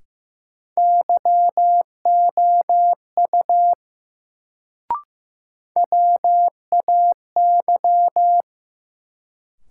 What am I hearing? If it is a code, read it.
Morse code, heard as a keyed tone, about 15 wpm, 700 Hz. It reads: YOU WAY
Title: Sentences from Top 100 Words - New Word Drill - 15wpm